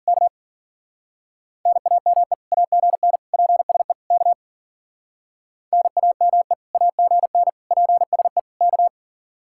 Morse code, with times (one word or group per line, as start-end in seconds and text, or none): 0.1-0.3 K
1.7-2.3 NAME
2.5-3.2 AGN
3.3-3.9 PSE
4.1-4.3 K
5.7-6.5 NAME
6.7-7.5 AGN
7.7-8.4 PSE
8.6-8.9 K